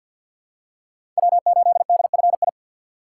Morse code, heard as a keyed tone, about 50 wpm, 700 Hz. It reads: W8DFI
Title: USA Call Signs - 4x Character Spacing - 50wpm